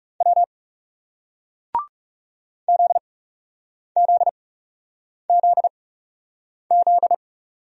Morse code, heard as a keyed tone, about 40 wpm, spaced as for 8 wpm, 700 Hz. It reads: W ZZZZ